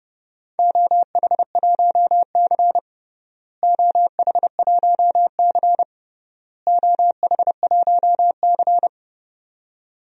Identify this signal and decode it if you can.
Morse code, heard as a keyed tone, about 30 wpm, 700 Hz. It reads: OH1C OH1C OH1C
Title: DXCC-100 Call Signs - 4 Characters - Repeated 3x - 30wpm